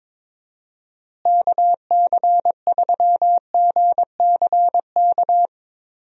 Morse code, heard as a keyed tone, about 22 wpm, 700 Hz. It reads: KC3GCK
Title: USA Call Signs - 6 Characters - Repeated 3x - 22wpm